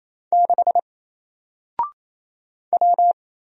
Morse code, heard as a keyed tone, about 28 wpm, 700 Hz. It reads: 6 W
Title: Single Letter-Number 28wpm